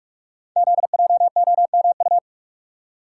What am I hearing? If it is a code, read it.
Morse code, heard as a keyed tone, about 44 wpm, 690 Hz. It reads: ZJOMU
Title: Code Groups - 5 Letters - 45wpm